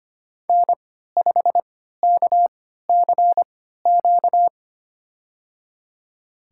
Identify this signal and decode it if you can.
Morse code, heard as a keyed tone, about 25 wpm, spaced as for 14 wpm, 700 Hz. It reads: N5KCQ
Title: USA Call Signs - 5 Characters - 3x Character Spacing - 25wpm